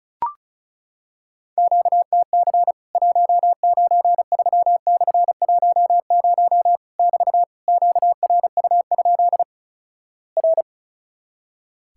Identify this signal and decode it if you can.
Morse code, heard as a keyed tone, about 36 wpm, 700 Hz. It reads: QTC 193/10 <BT> QRU? R
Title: Worked All Europe DX Contest - QTC Exchange - 35wpm